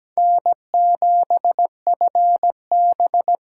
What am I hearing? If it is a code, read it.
Morse code, heard as a keyed tone, about 17 wpm, 700 Hz. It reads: N7FB
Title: USA Call Signs - 5 Characters - 8x Character Spacing - 17wpm